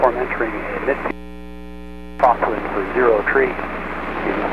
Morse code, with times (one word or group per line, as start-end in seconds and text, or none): none